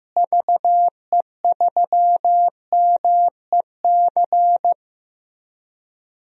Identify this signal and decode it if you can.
Morse code, heard as a keyed tone, about 15 wpm, 700 Hz. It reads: VE3MEC
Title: DXCC-100 Call Signs - 6 Characters - Repeated 3x - 15wpm